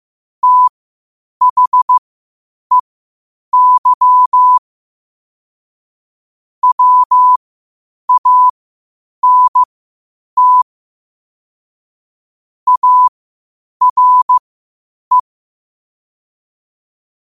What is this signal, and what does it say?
Morse code, heard as a keyed tone, about 15 wpm, spaced as for 8 wpm, 1000 Hz. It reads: THEY WANT ARE
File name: qotdfiles / 8wpm